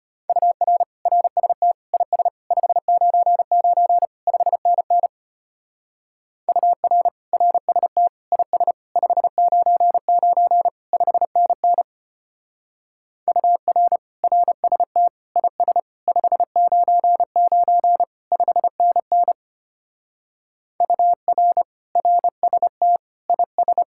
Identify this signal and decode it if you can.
Morse code, heard as a keyed tone, about 30 wpm, 700 Hz. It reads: UR RST IS 599 5NN UR RST IS 599 5NN UR RST IS 599 5NN UR RST IS
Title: Speed-Racing Common QSO Phrases Encore - 25wpm